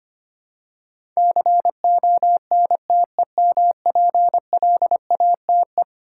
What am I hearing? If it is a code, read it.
Morse code, heard as a keyed tone, about 25 wpm, 700 Hz. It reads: CONTEMPLATE
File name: practice / 25wpm / Eleven Letter Words - 25wpm